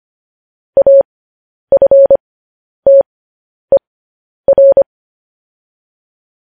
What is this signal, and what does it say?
Morse code, heard as a keyed tone, about 25 wpm, spaced as for 10 wpm, 550 Hz. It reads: AFTER
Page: News Headlines in Morse Code at 10 WPM